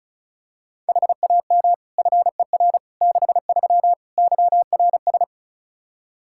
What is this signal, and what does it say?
Morse code, heard as a keyed tone, about 36 wpm, 700 Hz. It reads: HAM FER 63 YRS
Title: Practice QSO - Exchange 3 - More Info - 35wpm